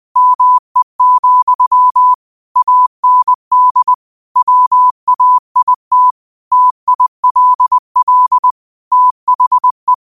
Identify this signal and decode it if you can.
Morse code, heard as a keyed tone, about 20 wpm, 1000 Hz. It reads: ME, AND WAIT TILL THE
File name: qotdfiles / 20wpm